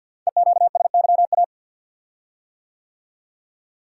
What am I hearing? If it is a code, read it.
Morse code, heard as a keyed tone, about 48 wpm, 700 Hz. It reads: EQSYA
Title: Code Groups - 5 Letters - 50wpm